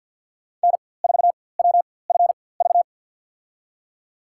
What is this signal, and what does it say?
Morse code, heard as a keyed tone, about 51 wpm, spaced as for 23 wpm, 700 Hz. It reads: N4WFV
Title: USA Call Signs - 4x Character Spacing - 50wpm